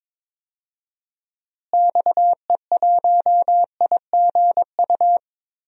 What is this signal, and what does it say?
Morse code, heard as a keyed tone, about 22 wpm, 700 Hz. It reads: XE1IGU
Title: DXCC-100 Call Signs - 6 Characters - Repeated 3x - 22wpm